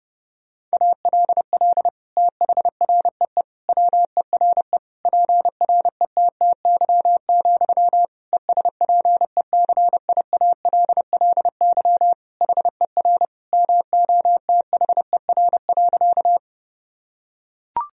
Morse code, heard as a keyed tone, about 32 wpm, 700 Hz. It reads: ALL THREE WERE PRETTY, ESPECIALLY HER MOTHER.
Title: Sentences from Top 1000 Words - Review Part 4 - 30wpm